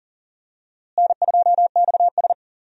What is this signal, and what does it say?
Morse code, heard as a keyed tone, about 39 wpm, 700 Hz. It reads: N2XS
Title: USA Call Signs - 4 Characters - Repeated 2x - 40wpm